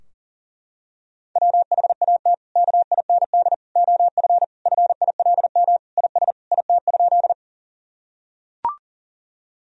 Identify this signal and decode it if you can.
Morse code, heard as a keyed tone, about 40 wpm, 700 Hz. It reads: WHAT KIND OF FILM IS IT?
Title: Sentences from Top 1000 Words - Review Part 3 - 40wpm